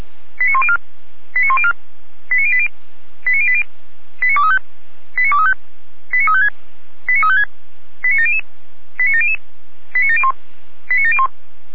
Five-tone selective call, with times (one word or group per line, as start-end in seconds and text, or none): 0.4-0.8 89104
1.4-1.7 89104
2.3-2.7 89080
3.3-3.6 89080
4.2-4.6 89235
5.2-5.5 89235
6.1-6.5 89356
7.1-7.4 89356
8.0-8.4 89700
9.0-9.4 89700
9.9-10.3 89701
10.9-11.3 89701